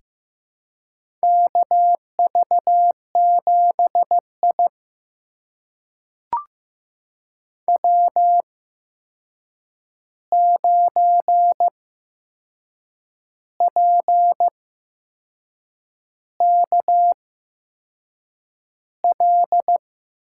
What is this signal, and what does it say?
Morse code, heard as a keyed tone, about 15 wpm, spaced as for 6 wpm, 700 Hz. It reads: KV7I W9PKL
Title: USA Call Signs - 8x Character Spacing - 15wpm